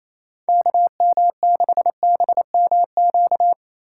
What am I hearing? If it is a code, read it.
Morse code, heard as a keyed tone, about 27 wpm, 700 Hz. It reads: KM6BMQ
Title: USA Call Signs - 6 Characters - 4x Character Spacing - 28wpm